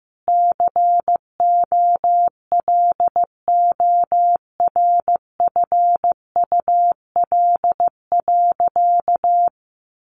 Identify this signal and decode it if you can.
Morse code, heard as a keyed tone, about 15 wpm, 700 Hz. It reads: COLORFUL.
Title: Easy Phrases 15wpm